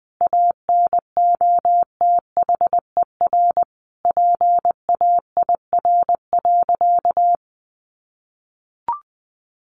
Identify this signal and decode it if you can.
Morse code, heard as a keyed tone, about 20 wpm, 700 Hz. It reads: ANOTHER PAIR.
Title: Sentences from Top 1200 Words 20wpm